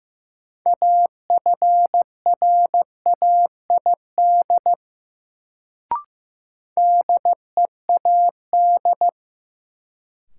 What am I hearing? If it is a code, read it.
Morse code, heard as a keyed tone, about 15 wpm, 700 Hz. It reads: AFRAID DEAD
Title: Sentences from Top 1500 Words - New Word Drill - 15wpm